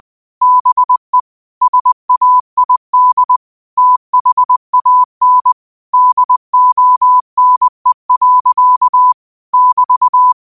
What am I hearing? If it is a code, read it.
Morse code, heard as a keyed tone, about 20 wpm, 1000 Hz. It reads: BE SAID THAN DONE. <BT>
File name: qotdfiles / 20wpm